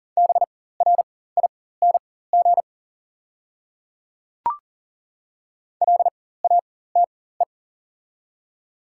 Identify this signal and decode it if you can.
Morse code, heard as a keyed tone, about 39 wpm, spaced as for 19 wpm, 690 Hz. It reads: BRING LATE
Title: 4x Character Spacing - Top 500 Words - 40wpm